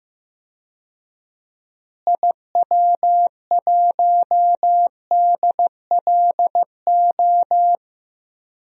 Morse code, heard as a keyed tone, about 16 wpm, 700 Hz. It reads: IW1DLO
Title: DXCC-100 Call Signs - 6 Characters - Repeated 2x - 15wpm